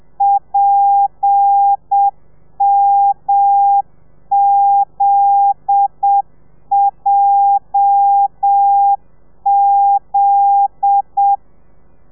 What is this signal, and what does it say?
Morse code, heard as a keyed tone, about 7 wpm, 800 Hz. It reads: PMZJZ